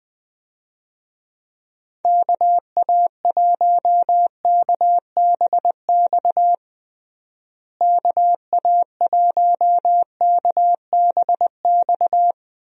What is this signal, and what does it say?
Morse code, heard as a keyed tone, about 20 wpm, 700 Hz. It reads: KA1KBX KA1KBX
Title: USA Call Signs - 6 Characters - Repeated 2x - 20wpm